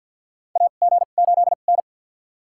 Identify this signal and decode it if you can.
Morse code, heard as a keyed tone, about 50 wpm, spaced as for 36 wpm, 700 Hz. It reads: AG8N
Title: USA Call Signs - 2x Character Spacing - 50wpm